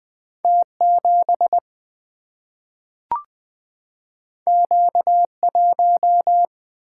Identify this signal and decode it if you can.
Morse code, heard as a keyed tone, about 20 wpm, 700 Hz. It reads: T7 Q1
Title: Single-Letter Single-Number 20wpm